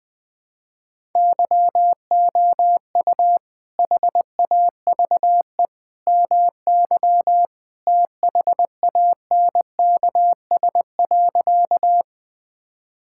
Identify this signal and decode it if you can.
Morse code, heard as a keyed tone, about 20 wpm, 700 Hz. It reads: YOU HAVE MY THANKS.
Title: Sentences with 2-Word Phrases - 20wpm